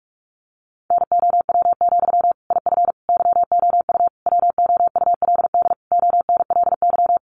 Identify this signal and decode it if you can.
Morse code, heard as a keyed tone, about 44 wpm, 700 Hz. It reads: NOW, IF YOU WOULD ONLY
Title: Sentences from Top 900 Words 45wpm